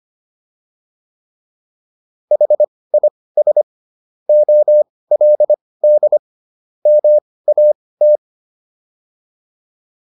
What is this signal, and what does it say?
Morse code, heard as a keyed tone, about 25 wpm, spaced as for 18 wpm, 600 Hz. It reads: HIS OLD MAT
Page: intermediate-practice-files